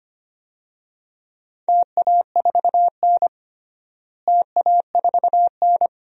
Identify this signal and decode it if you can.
Morse code, heard as a keyed tone, about 25 wpm, 700 Hz. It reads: TA4N TA4N
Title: DXCC-100 Call Signs - 4 Characters - Repeated 3x - 25wpm